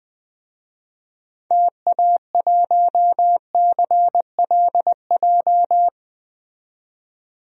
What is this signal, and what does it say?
Morse code, heard as a keyed tone, about 20 wpm, 700 Hz. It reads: TA1CLJ
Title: DXCC-100 Call Signs - 6 Characters - 20wpm